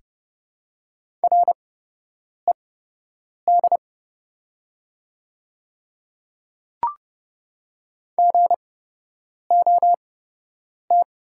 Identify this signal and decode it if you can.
Morse code, heard as a keyed tone, about 31 wpm, spaced as for 8 wpm, 700 Hz. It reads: RED GOT
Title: 8x Character Spacing - Top 500 Words - 30wpm